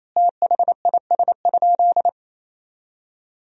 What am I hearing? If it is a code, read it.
Morse code, heard as a keyed tone, about 28 wpm, 700 Hz. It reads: THIS?
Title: 4x Word-Spacing - Sentences from Top 100 Words - 28wpm